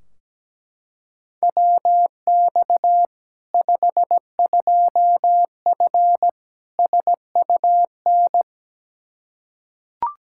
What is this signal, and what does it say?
Morse code, heard as a keyed tone, about 17 wpm, 700 Hz. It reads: WX 52F SUN
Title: Practice QSO - Exchange 2 - Info - 17wpm